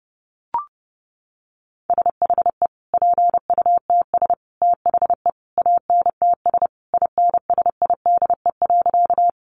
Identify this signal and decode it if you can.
Morse code, heard as a keyed tone, about 30 wpm, 700 Hz. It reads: SHE PUTS THE ANTS INSIDE.
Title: Additional Easy Phrases 30wpm